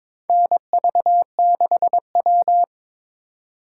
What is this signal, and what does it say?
Morse code, heard as a keyed tone, about 21 wpm, 700 Hz. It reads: NV6W
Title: USA Call Signs - 4 Characters - 3x Character Spacing - 22wpm